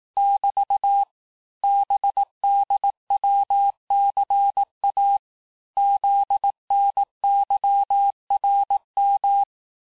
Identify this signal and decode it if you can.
Morse code, heard as a keyed tone, about 18 wpm, 800 Hz. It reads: <BT> BDWCA ZNYRM
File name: cours-du-lundi files / cours-1852-texte-chiffre-vitesse-17MPM